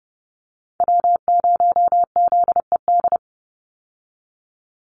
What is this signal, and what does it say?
Morse code, heard as a keyed tone, about 30 wpm, 700 Hz. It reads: W0ZED
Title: USA Call Signs 30wpm